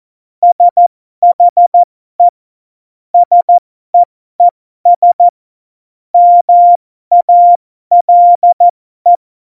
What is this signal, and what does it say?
Morse code, heard as a keyed tone, about 14 wpm, spaced as for 12 wpm, 700 Hz. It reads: SHE SEES MALE
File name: Morsekurs / Witze